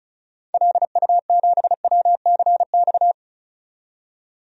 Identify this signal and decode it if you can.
Morse code, heard as a keyed tone, about 34 wpm, 700 Hz. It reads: LU7WCX
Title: DXCC-100 Call Signs - 6 Characters - 35wpm